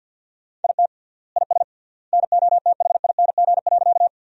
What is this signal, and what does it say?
Morse code, heard as a keyed tone, about 50 wpm, 700 Hz. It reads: IT IS NOTHING.
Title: 3x Word-Spacing - Sentences from Top 500 Words - 50wpm